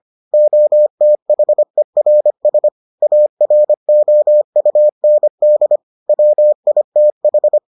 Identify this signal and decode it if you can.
Morse code, heard as a keyed tone, about 25 wpm, 600 Hz. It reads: OTHERS AROUND WITH